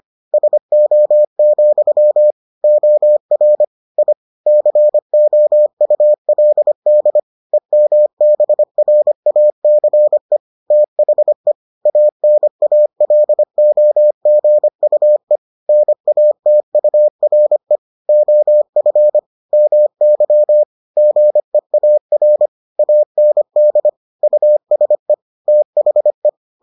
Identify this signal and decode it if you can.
Morse code, heard as a keyed tone, about 25 wpm, 600 Hz. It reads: SO, OR I COULD EMBRACE THE ANALOGUE NATURE OF MY GEAR AND USE THE